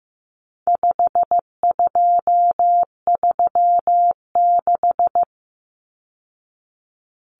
Morse code, heard as a keyed tone, about 15 wpm, 700 Hz. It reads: 5236
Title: Numbers - 4-Digit 15wpm